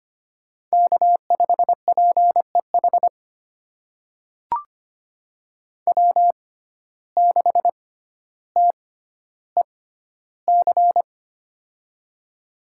Morse code, heard as a keyed tone, about 25 wpm, spaced as for 13 wpm, 700 Hz. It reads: K5PEH W6TEC
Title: USA Call Signs - 5 Characters - 6x Character Spacing - 25wpm